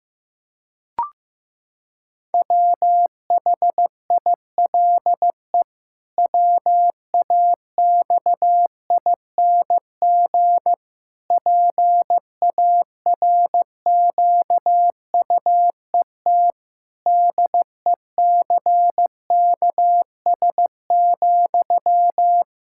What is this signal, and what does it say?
Morse code, heard as a keyed tone, about 15 wpm, 700 Hz. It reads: WHILE WAXING PARQUET DECKS,
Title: Pangram Challenge v2 - 15wpm